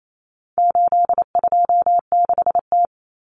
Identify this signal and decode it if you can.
Morse code, heard as a keyed tone, about 28 wpm, 700 Hz. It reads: 826T
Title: Common Cut Numbers - 4-Digit 28wpm